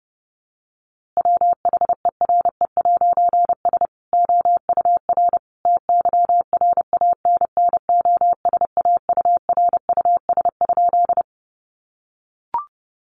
Morse code, heard as a keyed tone, about 30 wpm, 700 Hz. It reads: WHERE'S OUR TYRANNOSAURUS?
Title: A Sound Of Thunder 30wpm